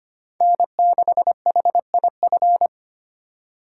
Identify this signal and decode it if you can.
Morse code, heard as a keyed tone, about 25 wpm, 700 Hz. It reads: N6HIF
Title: USA Call Signs - 5 Characters - 4x Character Spacing - 25wpm